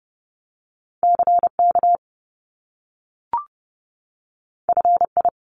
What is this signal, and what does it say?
Morse code, heard as a keyed tone, about 30 wpm, 700 Hz. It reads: CK FI
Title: Top Two Letter Combinations in Words 30wpm